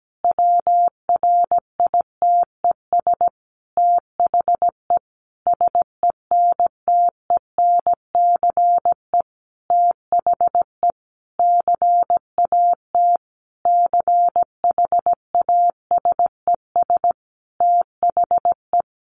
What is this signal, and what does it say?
Morse code, heard as a keyed tone, about 17 wpm, 700 Hz. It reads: WRITES THE SENTENCE THE CAT CHASES THE
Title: Additional Easy Phrases 17wpm